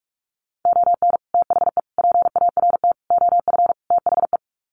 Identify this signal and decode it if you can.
Morse code, heard as a keyed tone, about 44 wpm, 700 Hz. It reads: ON THE PART OF THE
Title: Sets of 5 Words 45wpm